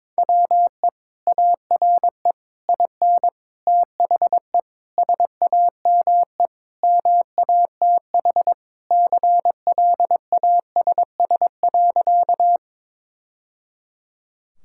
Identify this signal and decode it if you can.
Morse code, heard as a keyed tone, about 22 wpm, 700 Hz. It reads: WE ARE IN THE SAME MATH CLASS.
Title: Sentences from Top 1000 Words - Review Part 6 - 22wpm